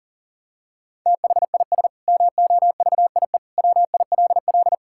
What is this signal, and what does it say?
Morse code, heard as a keyed tone, about 40 wpm, 700 Hz. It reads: THIS MOVIE WILL
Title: Sentences from Top 1000 Words - Review Part 5 - 40wpm